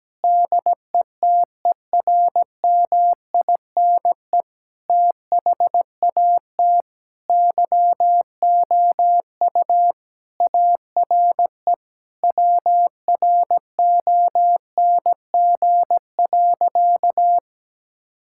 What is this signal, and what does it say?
Morse code, heard as a keyed tone, about 17 wpm, 700 Hz. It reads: DETERMINE THAT YOU ARE WRONG.
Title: Sentences from Top 1000 Words - Review Part 5 - 17wpm